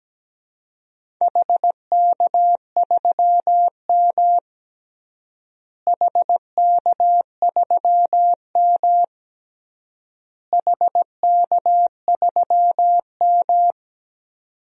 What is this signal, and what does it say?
Morse code, heard as a keyed tone, about 17 wpm, 700 Hz. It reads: HK3M HK3M HK3M
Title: DXCC-100 Call Signs - 4 Characters - Repeated 3x - 17wpm